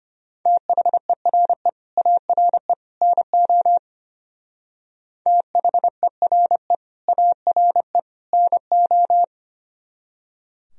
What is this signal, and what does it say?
Morse code, heard as a keyed tone, about 32 wpm, 700 Hz. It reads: THERE ARE NO THERE ARE NO